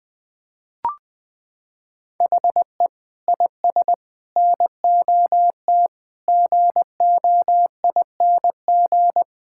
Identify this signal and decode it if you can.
Morse code, heard as a keyed tone, about 21 wpm, 700 Hz. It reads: HE IS NOT GOING
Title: Sentences with 6-Word Phrases - 20wpm